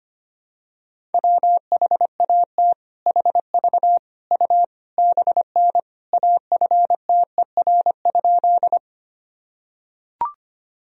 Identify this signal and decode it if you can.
Morse code, heard as a keyed tone, about 25 wpm, 700 Hz. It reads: WHAT HV U BN AFTER?
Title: Sentences from Top 200 Words and Abbreviations 25wpm